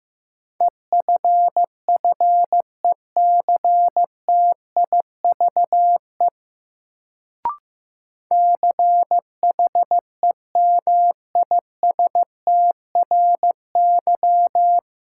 Morse code, heard as a keyed tone, about 15 wpm, 700 Hz. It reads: EFFECTIVE CHEMISTRY
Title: Sentences from Top 1000 Words - Review Part 6 - New Word Drill - 15wpm